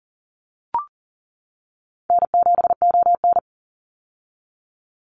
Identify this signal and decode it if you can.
Morse code, heard as a keyed tone, about 40 wpm, 700 Hz. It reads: N7ON
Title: Easy Call Signs - 40wpm